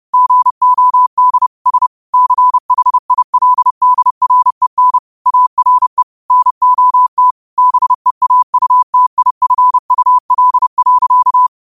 Morse code, heard as a keyed tone, about 31 wpm, 1000 Hz. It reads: GOD S CHILDREN ARE NOT BEAUTIFUL.
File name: qotdfiles / 30wpm